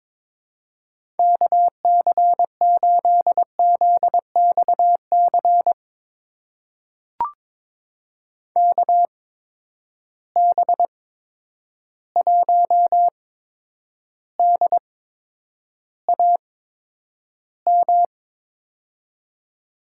Morse code, heard as a keyed tone, about 22 wpm, spaced as for 9 wpm, 700 Hz. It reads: KC8ZXC KB1DAM
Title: USA Call Signs - 6 Characters - 8x Character Spacing - 22wpm